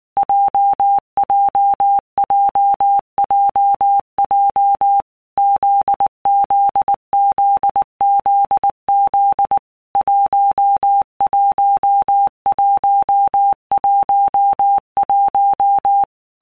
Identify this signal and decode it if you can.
Morse code, heard as a keyed tone, about 19 wpm, 800 Hz. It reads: JJJJJ ZZZZZ 11111
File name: All Groups 1-minute Blocks / Groups@18 / JZ17